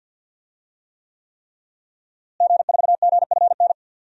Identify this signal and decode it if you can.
Morse code, heard as a keyed tone, about 47 wpm, 700 Hz. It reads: G4GFN
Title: DXCC-100 Call Signs - 5 Characters - Repeated 3x - 50wpm